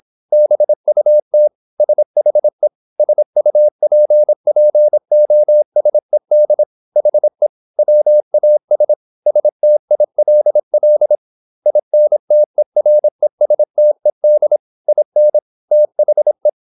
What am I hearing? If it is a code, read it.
Morse code, heard as a keyed tone, about 26 wpm, 600 Hz. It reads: BUT SHE SUPPOSED HE WAS STILL INTERESTED IN THE
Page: Never 26wpm | CW med Gnister